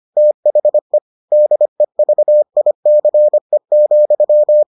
Morse code, heard as a keyed tone, about 25 wpm, 600 Hz. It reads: THE DEVICE,